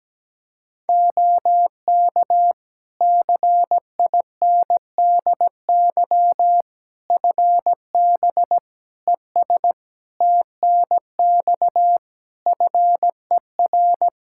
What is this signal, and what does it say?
Morse code, heard as a keyed tone, about 17 wpm, 700 Hz. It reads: OK CINDY FB ES TNX FER
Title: K1USN SST QSO Practice 17wpm